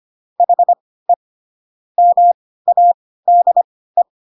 Morse code, heard as a keyed tone, about 25 wpm, spaced as for 16 wpm, 700 Hz. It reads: HE MADE
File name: Pinocchio / Pinocchio ch3 10 25 ik1hge